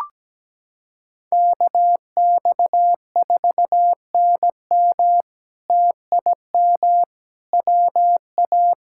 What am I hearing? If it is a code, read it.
Morse code, heard as a keyed tone, about 17 wpm, 700 Hz. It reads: KX4NM TIM WA
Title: K1USN SST QSO Practice - 17wpm